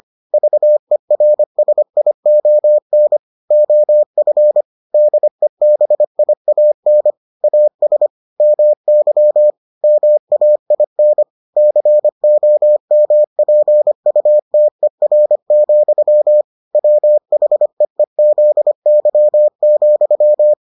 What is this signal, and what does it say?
Morse code, heard as a keyed tone, about 25 wpm, 600 Hz. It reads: VERSION OF DEBIAN AS MY MAIN COMPUTER, WHEEZY,